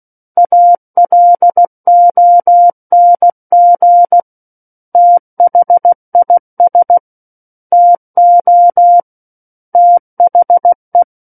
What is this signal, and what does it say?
Morse code, heard as a keyed tone, about 16 wpm, 700 Hz. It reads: ALONG THIS TO THE
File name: Book One / 07-Chapter 7 - 16 WPM